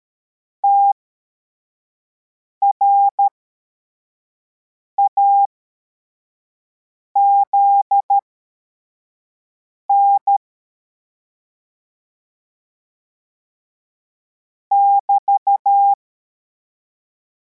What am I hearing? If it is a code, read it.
Morse code, heard as a keyed tone, about 13 wpm, spaced as for 4 wpm, 800 Hz. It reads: TRAZN <BT>
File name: formation / cw / Text43